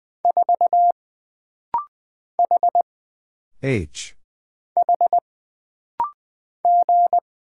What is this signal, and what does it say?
Morse code, heard as a keyed tone, about 20 wpm, 700 Hz. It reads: 4 H H G
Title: Single Letter-Number - Rapid-Fire 20wpm